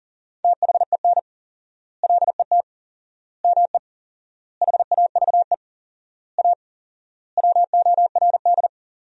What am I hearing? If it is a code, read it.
Morse code, heard as a keyed tone, about 40 wpm, 700 Hz. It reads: THEN LET ME HAVE A WORD
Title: 4x Word-Spacing - Sentences from Top 300 Words - 40wpm